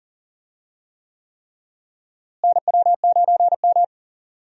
Morse code, heard as a keyed tone, about 40 wpm, 700 Hz. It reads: NW9M
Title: USA Call Signs - 4 Characters - Repeated 3x - 40wpm